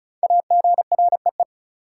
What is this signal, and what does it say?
Morse code, heard as a keyed tone, about 36 wpm, 700 Hz. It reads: AGREE
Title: Five Letter Words - 35wpm